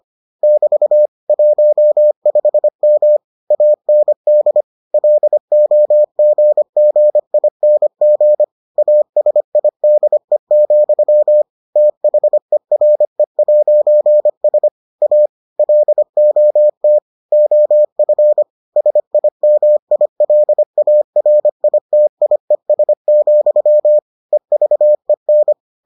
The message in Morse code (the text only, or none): <BT> 15M AND LOGGING ASIDE, THERE'S A LOT OF SIMILARITIES, EVEN